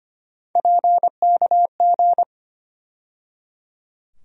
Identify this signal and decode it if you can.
Morse code, heard as a keyed tone, about 25 wpm, 700 Hz. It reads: PKG
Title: USA Call Sign Postfixes - 3 Letters 25wpm